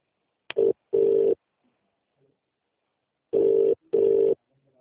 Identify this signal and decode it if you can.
Ringback tone (UK), 400 + 450 Hz.